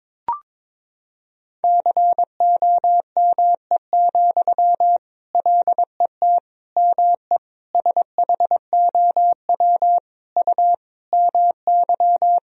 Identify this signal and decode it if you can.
Morse code, heard as a keyed tone, about 23 wpm, 700 Hz. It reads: COME, LET ME SHOW U MY
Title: Sentences from Top 500 Words and Abbreviations 22wpm